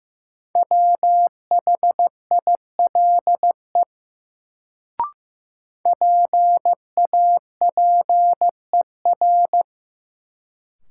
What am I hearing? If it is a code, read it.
Morse code, heard as a keyed tone, about 15 wpm, 700 Hz. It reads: WHILE PAPER
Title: Sentences from Top 300 Words - New Word Drill - 15wpm